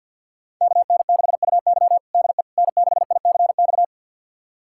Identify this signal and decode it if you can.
Morse code, heard as a keyed tone, about 51 wpm, 700 Hz. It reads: KN6UY DE N6ICX
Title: USA Callsign DE Callsign - 5 Characters - 50wpm